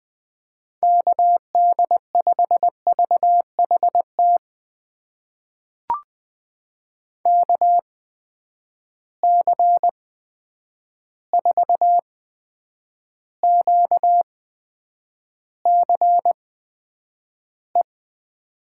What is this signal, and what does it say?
Morse code, heard as a keyed tone, about 20 wpm, spaced as for 9 wpm, 700 Hz. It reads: KD5VHT KC4QCE
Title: USA Call Signs - 6 Characters - 8x Character Spacing - 20wpm